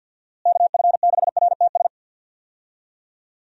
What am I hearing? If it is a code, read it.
Morse code, heard as a keyed tone, about 50 wpm, 700 Hz. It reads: KF6RTS
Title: USA Call Signs - 6 Characters - Repeated 2x - 50wpm